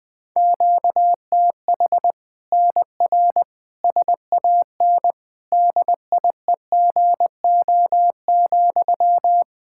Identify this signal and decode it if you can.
Morse code, heard as a keyed tone, about 21 wpm, 700 Hz. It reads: QTH NR SAN DIEGO,
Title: Practice QSO - Exchange 1 - Beginning - 20wpm